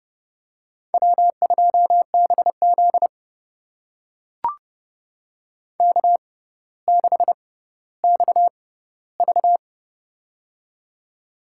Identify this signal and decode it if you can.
Morse code, heard as a keyed tone, about 29 wpm, spaced as for 15 wpm, 700 Hz. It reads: W2BZ K6XV
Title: USA Call Signs - 4 Characters - 6x Character Spacing - 30wpm